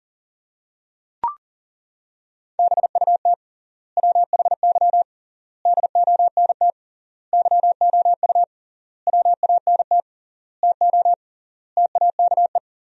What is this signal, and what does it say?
Morse code, heard as a keyed tone, about 40 wpm, 700 Hz. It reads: BUT WHY DONT YOU WANT TO TAKE
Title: 3x Word-Spacing - Sentences from Top 300 Words - 40wpm